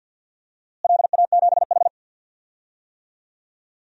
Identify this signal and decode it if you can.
Morse code, heard as a keyed tone, about 48 wpm, 700 Hz. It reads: LA7H
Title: DXCC-100 Call Signs - 4 Characters - Repeated 3x - 50wpm